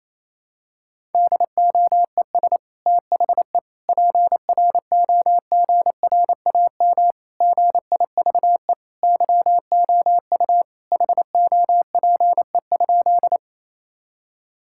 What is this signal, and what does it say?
Morse code, heard as a keyed tone, about 28 wpm, 700 Hz. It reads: DOES THE PROGRAM GIVE YOU HOPE?
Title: Sentences from Top 1000 Words - Review Part 5 - 28wpm